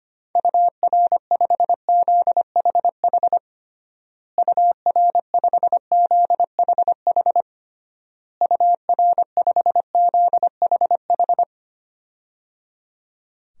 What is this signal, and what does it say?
Morse code, heard as a keyed tone, about 25 wpm, 700 Hz. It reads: UR5ZHH UR5ZHH UR5ZHH
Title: DXCC-100 Call Signs - 6 Characters - Repeated 3x - 25wpm